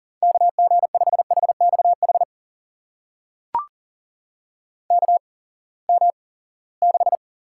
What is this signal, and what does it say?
Morse code, heard as a keyed tone, about 39 wpm, spaced as for 22 wpm, 700 Hz. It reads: KG5HXH KM6
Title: USA Call Signs - 6 Characters - 8x Character Spacing - 40wpm